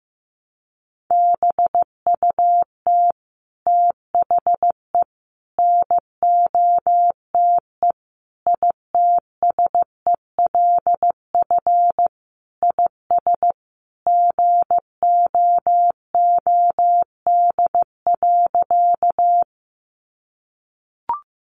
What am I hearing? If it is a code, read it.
Morse code, heard as a keyed tone, about 15 wpm, 700 Hz. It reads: BUT THE NOTE ITSELF IS GOOD.
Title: Sentences from Top 800 Words 15wpm